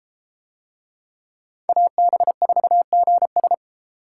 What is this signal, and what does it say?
Morse code, heard as a keyed tone, about 30 wpm, 700 Hz. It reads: AB4GS